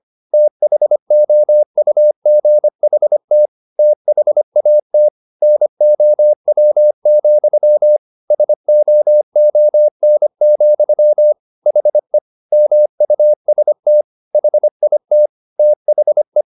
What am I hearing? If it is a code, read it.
Morse code, heard as a keyed tone, about 25 wpm, 600 Hz. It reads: THOUGHT THAT NOW, SOON, HE MUST HIT THE